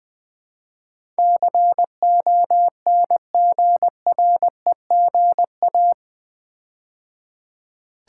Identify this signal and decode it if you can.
Morse code, heard as a keyed tone, about 21 wpm, 700 Hz. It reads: CONGREGA
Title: Build A Word - 4 Letters At A Time - 12-Letter Words - 20wpm